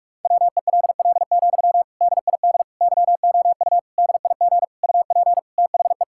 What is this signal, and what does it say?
Morse code, heard as a keyed tone, about 46 wpm, 700 Hz. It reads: WELL, DID YOU DIG UP THE
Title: Sentences from Top 1600 Words 45wpm